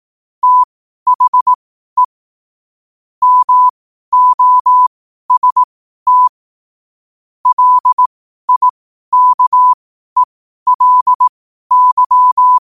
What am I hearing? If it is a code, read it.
Morse code, heard as a keyed tone, about 18 wpm, spaced as for 12 wpm, 1000 Hz. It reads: THE MOST LIKELY
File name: qotdfiles / 12wpm